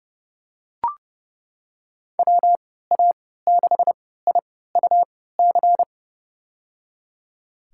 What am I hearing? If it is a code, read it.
Morse code, heard as a keyed tone, about 30 wpm, spaced as for 17 wpm, 700 Hz. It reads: WA6IUC
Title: USA Call Signs - 6 Characters - 3x Character Spacing - 30wpm